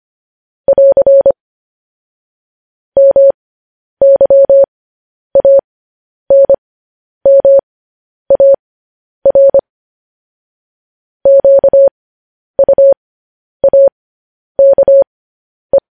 Morse code, heard as a keyed tone, about 25 wpm, spaced as for 10 wpm, 550 Hz. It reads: <AR> MYANMAR QUAKE
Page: news-headlines-in-morse-code-at-10-wpm